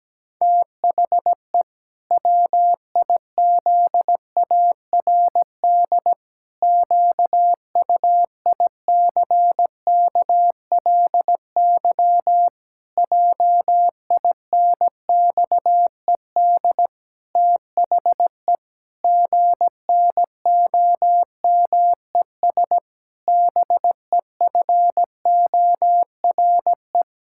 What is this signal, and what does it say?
Morse code, heard as a keyed tone, about 17 wpm, 700 Hz. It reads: THE WIZARD QUICKLY JINXED THE GNOMES BEFORE
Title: Pangram Challenge v2 - 17wpm